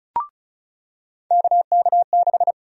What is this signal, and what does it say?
Morse code, heard as a keyed tone, about 36 wpm, 700 Hz. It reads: KK6
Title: USA Call Sign Prefixes - 35wpm